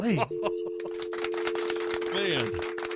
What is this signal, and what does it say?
Telephone call progress tone (North America), heard as a telephone network's tone, about 350, 440 Hz, with speech. Dial tone.